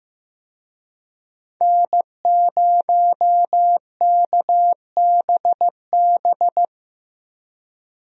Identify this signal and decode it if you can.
Morse code, heard as a keyed tone, about 15 wpm, 700 Hz. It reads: N0KBB
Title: USA Call Signs - 5 Characters - Repeated 2x - 15wpm